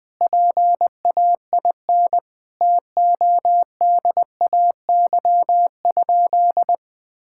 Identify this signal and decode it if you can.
Morse code, heard as a keyed tone, about 20 wpm, 700 Hz. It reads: PAIN TODAY?
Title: Sentences from Top 1600 Words 20wpm